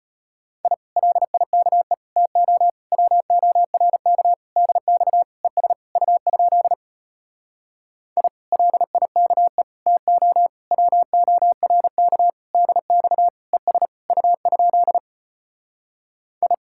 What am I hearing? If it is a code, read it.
Morse code, heard as a keyed tone, about 33 wpm, 700 Hz. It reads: I LIKE TO WORK DX ES U? I LIKE TO WORK DX ES U? I